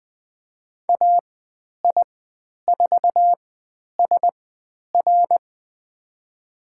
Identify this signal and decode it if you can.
Morse code, heard as a keyed tone, about 20 wpm, spaced as for 10 wpm, 700 Hz. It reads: AI4SR
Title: 102 Lesson 17g - Letter u - Call Signs 20wpm